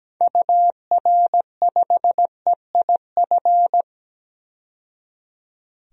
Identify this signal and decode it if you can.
Morse code, heard as a keyed tone, about 17 wpm, 700 Hz. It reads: UR5EIF
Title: DXCC-100 Call Signs - 6 Characters - 17wpm